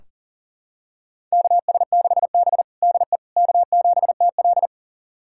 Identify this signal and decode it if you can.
Morse code, heard as a keyed tone, about 40 wpm, 700 Hz. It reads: KS6B DE K7TL
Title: USA Callsign DE Callsign - 4 Characters - 40wpm